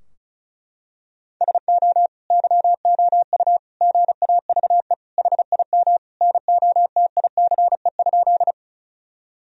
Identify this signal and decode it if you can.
Morse code, heard as a keyed tone, about 36 wpm, 700 Hz. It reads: SO YOU GAVE HIM NOTICE?
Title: Sentences from Top 1000 Words - Review Part 2 - 35wpm